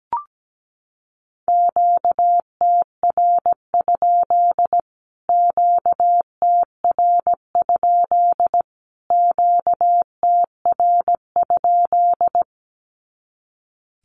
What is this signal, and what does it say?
Morse code, heard as a keyed tone, about 17 wpm, 700 Hz. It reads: QTR? QTR? QTR?
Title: Q-Codes Repeated 3x 17wpm